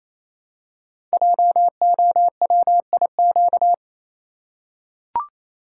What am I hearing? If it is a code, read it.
Morse code, heard as a keyed tone, about 28 wpm, 700 Hz. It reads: JOWIQ
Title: Code Groups - 5 Letters - 28wpm